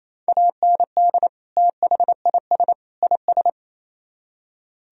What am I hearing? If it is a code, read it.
Morse code, heard as a keyed tone, about 34 wpm, 700 Hz. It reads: AND THIS IS
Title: Speed-Racing - Sets of 3 Words 28wpm